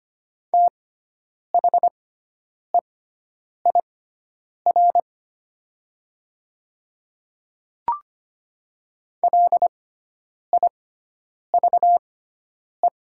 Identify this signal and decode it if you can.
Morse code, heard as a keyed tone, about 25 wpm, spaced as for 9 wpm, 690 Hz. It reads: THEIR LIVE
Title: 6x Character Spacing - Top 200 Words - 25wpm